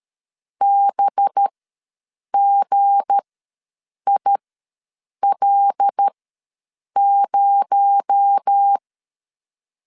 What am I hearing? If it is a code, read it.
Morse code, heard as a keyed tone, about 13 wpm, spaced as for 7 wpm, 800 Hz. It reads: BGIL0
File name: LEZIONI / 7 GF9Z / 7 G ws 12wpm 3140